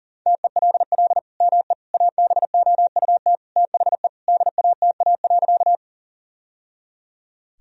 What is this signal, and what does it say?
Morse code, heard as a keyed tone, about 37 wpm, 700 Hz. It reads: TELL ME ABOUT THE DATA.
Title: Sentences from Top 1000 Words - Review Part 3 - 40wpm